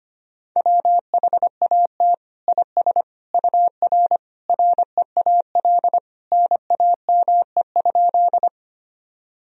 Morse code, heard as a keyed tone, about 25 wpm, 700 Hz. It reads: WHAT IS UR REAL NAME?
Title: Sentences from Top 300 Words and Abbreviations 25wpm